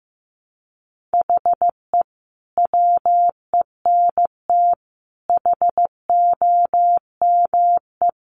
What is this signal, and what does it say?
Morse code, heard as a keyed tone, about 15 wpm, 700 Hz. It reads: HE WENT HOME
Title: Sentences from Top 1300 Words 15wpm